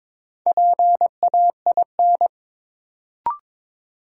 Morse code, heard as a keyed tone, about 22 wpm, 700 Hz. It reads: PAIN
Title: Sentences from Top 1600 Words - New Word Drill - 22wpm